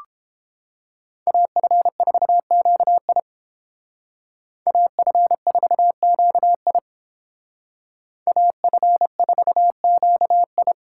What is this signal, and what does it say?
Morse code, heard as a keyed tone, about 28 wpm, 700 Hz. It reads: AF4QI AF4QI AF4QI